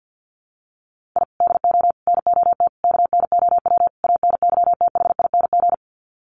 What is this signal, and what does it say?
Morse code, heard as a keyed tone, about 50 wpm, 700 Hz. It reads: I DO NOT KNOW ANYTHING
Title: Sets of 5 Words Encore 50wpm